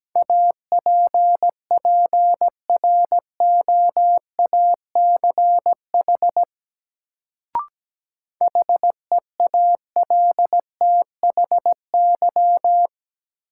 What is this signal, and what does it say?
Morse code, heard as a keyed tone, about 17 wpm, 700 Hz. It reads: APPROACH HEALTHY
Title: Sentences from Top 1400 Words - New Word Drill Alt - 17wpm